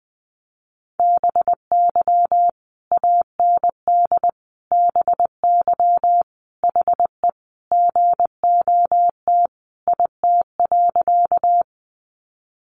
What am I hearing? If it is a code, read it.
Morse code, heard as a keyed tone, about 20 wpm, 700 Hz. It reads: BY AND BY HE GOT IT.
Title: Sentences from Top 300 Words 20wpm